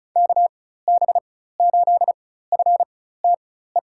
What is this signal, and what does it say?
Morse code, heard as a keyed tone, about 34 wpm, spaced as for 16 wpm, 700 Hz. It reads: KB8FTE
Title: USA Call Signs - 6 Characters - 4x Character Spacing - 35wpm